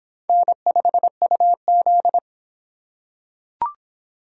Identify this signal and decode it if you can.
Morse code, heard as a keyed tone, about 25 wpm, 700 Hz. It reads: N5UZ